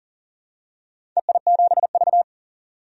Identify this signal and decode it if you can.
Morse code, heard as a keyed tone, about 40 wpm, 700 Hz. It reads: EI7V
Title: DXCC-100 Call Signs - 4 Characters - 40wpm